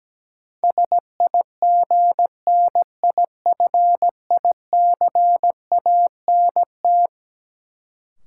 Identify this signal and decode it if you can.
Morse code, heard as a keyed tone, about 17 wpm, 700 Hz. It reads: SIGNIFICANT
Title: Sentences from Top 1000 Words - Review Part 6 - New Word Drill - 17wpm